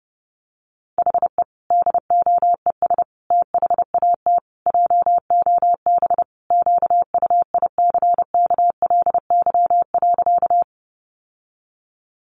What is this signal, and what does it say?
Morse code, heard as a keyed tone, about 30 wpm, 700 Hz. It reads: HE DOES THAT JOB QUICKLY.
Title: Sentences from Top 700 Words 30wpm